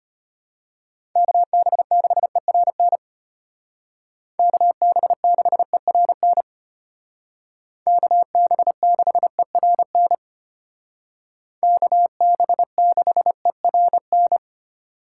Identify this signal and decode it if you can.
Morse code, heard as a keyed tone, about 33 wpm, 700 Hz. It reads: KB6ERN KB6ERN KB6ERN KB6ERN